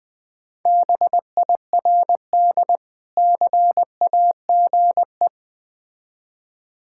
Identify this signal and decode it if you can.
Morse code, heard as a keyed tone, about 20 wpm, 700 Hz. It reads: BIRD CAGE
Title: Difficult Words with Hints - 20wpm